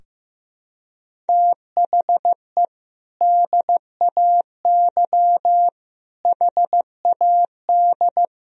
Morse code, heard as a keyed tone, about 15 wpm, 700 Hz. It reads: THE DAY HAD
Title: Sentences from Top 1000 Words - Review Part 2 - 15wpm